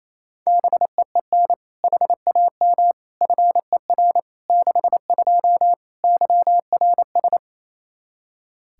Morse code, heard as a keyed tone, about 28 wpm, 700 Hz. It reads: BEEN HAM FER 62 YRS
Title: K1USN SST QSO Practice 28wpm